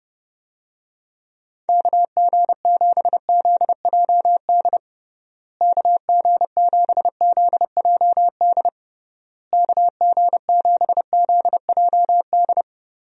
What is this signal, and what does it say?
Morse code, heard as a keyed tone, about 32 wpm, 700 Hz. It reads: KG7ZJD KG7ZJD KG7ZJD
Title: USA Call Signs - 6 Characters - Repeated 3x - 30wpm